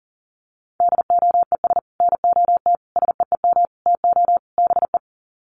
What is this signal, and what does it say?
Morse code, heard as a keyed tone, about 40 wpm, 700 Hz. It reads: DOES NOT SEEM TO BE
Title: Sets of 5 Words 40wpm